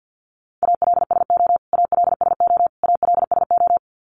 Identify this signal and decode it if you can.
Morse code, heard as a keyed tone, about 48 wpm, 700 Hz. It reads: ALSO ALSO ALSO
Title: Top 300 Words Repeated 50wpm